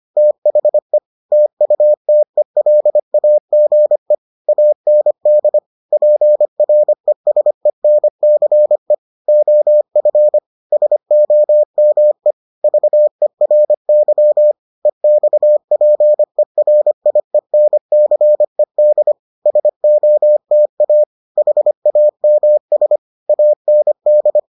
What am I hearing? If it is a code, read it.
Morse code, heard as a keyed tone, about 25 wpm, 600 Hz. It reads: THE TUTELAGE AND PRESENCE OF SOME VERY EXPERIENCED SOTA HAMS AND